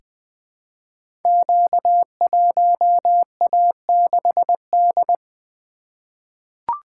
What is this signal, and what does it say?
Morse code, heard as a keyed tone, about 20 wpm, 700 Hz. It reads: Q1A6D
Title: Code Groups - 5 Letter:Numbers - 20wpm